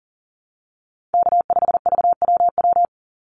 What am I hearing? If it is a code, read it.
Morse code, heard as a keyed tone, about 40 wpm, 700 Hz. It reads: K5VWW
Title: Call Signs - 40wpm